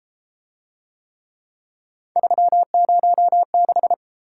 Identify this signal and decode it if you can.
Morse code, heard as a keyed tone, about 28 wpm, 700 Hz. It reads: 306